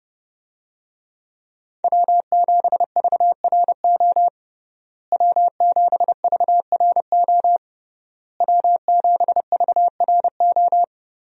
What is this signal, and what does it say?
Morse code, heard as a keyed tone, about 30 wpm, 700 Hz. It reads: W7VRO W7VRO W7VRO
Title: USA Call Signs - 5 Characters - Repeated 3x - 30wpm